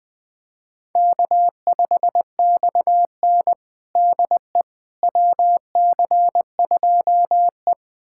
Morse code, heard as a keyed tone, about 21 wpm, 700 Hz. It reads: K5XN DE WC2E
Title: USA Callsign DE Callsign - 4 Characters - 20wpm